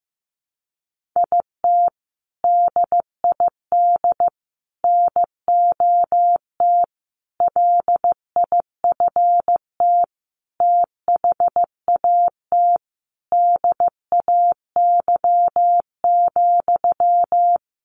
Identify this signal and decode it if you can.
Morse code, heard as a keyed tone, about 15 wpm, 700 Hz. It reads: IT DID NOT LIFT THAT DAY,
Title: Sentences from Top 1500 Words 15wpm